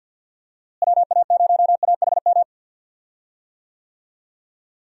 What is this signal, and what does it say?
Morse code, heard as a keyed tone, about 48 wpm, 700 Hz. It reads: WA0AHM
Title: USA Call Signs - 6 Characters - Repeated 3x - 50wpm